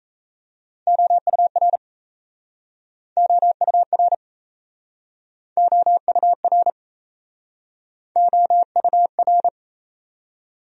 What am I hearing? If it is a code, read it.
Morse code, heard as a keyed tone, about 37 wpm, 700 Hz. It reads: OUR OUR OUR OUR